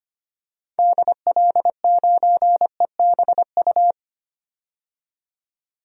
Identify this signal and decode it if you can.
Morse code, heard as a keyed tone, about 25 wpm, 700 Hz. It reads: DL9EBU
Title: DXCC-100 Call Signs - 6 Characters - 25wpm